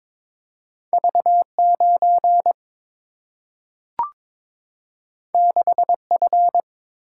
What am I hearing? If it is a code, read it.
Morse code, heard as a keyed tone, about 22 wpm, 700 Hz. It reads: V9 6F
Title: Single-Letter Single-Number 22wpm